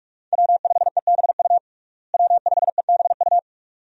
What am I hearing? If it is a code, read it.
Morse code, heard as a keyed tone, about 41 wpm, 700 Hz. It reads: W5EBU W5EBU
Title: USA Call Signs - 5 Characters - Repeated 2x - 45wpm